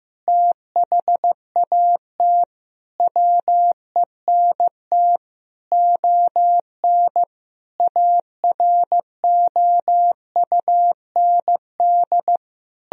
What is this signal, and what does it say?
Morse code, heard as a keyed tone, about 15 wpm, 700 Hz. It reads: THAT WENT ON AROUND
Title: Sentences from Top 1000 Words - Review Part 3 - 15wpm